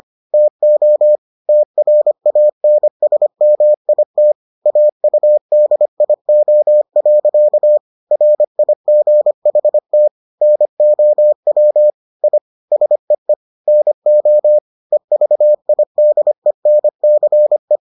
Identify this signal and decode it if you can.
Morse code, heard as a keyed tone, about 25 wpm, 600 Hz. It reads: TO TRANSMIT AUDIO. RIGHT NOW I SEE NO EVIDENCE